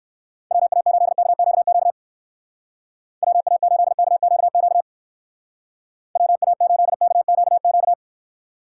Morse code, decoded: WA8KQQ WA8KQQ WA8KQQ

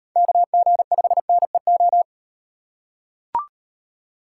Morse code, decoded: KG5NEO

